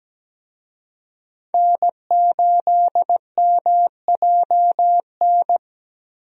Morse code, heard as a keyed tone, about 17 wpm, 700 Hz. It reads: N8MJN
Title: USA Call Signs - 5 Characters - Repeated 3x - 17wpm